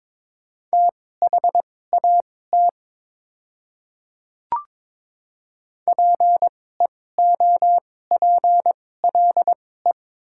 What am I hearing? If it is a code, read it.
Morse code, heard as a keyed tone, about 22 wpm, spaced as for 16 wpm, 700 Hz. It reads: THAT PEOPLE
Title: 2x Character Spacing - Top 100 Words - 22wpm